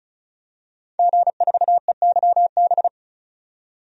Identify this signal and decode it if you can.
Morse code, heard as a keyed tone, about 35 wpm, 700 Hz. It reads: G4EYB
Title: DXCC-100 Call Signs - 5 Characters - 35wpm